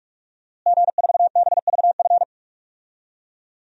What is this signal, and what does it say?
Morse code, heard as a keyed tone, about 46 wpm, 700 Hz. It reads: G4BVF
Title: DXCC-100 Call Signs - 5 Characters - Repeated 3x - 45wpm